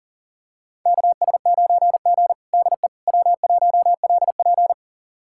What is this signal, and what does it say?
Morse code, heard as a keyed tone, about 40 wpm, 700 Hz. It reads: KS9G DE W1LP
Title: USA Callsign DE Callsign - 4 Characters - 40wpm